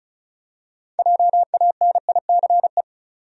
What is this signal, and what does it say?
Morse code, heard as a keyed tone, about 34 wpm, 700 Hz. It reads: JANICE
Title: USA Common Names - 35wpm